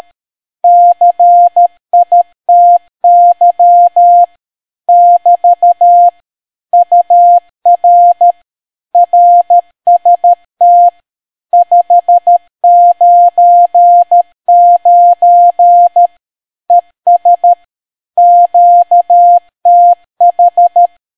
Morse code, heard as a keyed tone, about 13 wpm, 700 Hz. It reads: CITY <BT> UR RST 599 ES QTH